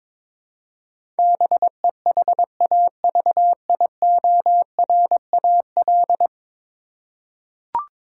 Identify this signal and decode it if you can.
Morse code, heard as a keyed tone, about 22 wpm, 700 Hz. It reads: BEHAVIORAL
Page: morse-code-ninja-practice-22wpm